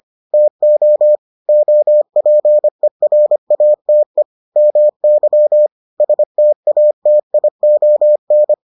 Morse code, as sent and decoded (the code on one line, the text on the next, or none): - --- / --- .--. .-. .- - . / -- -.-- / ... - .- - .. --- -.
TO OPERATE MY STATION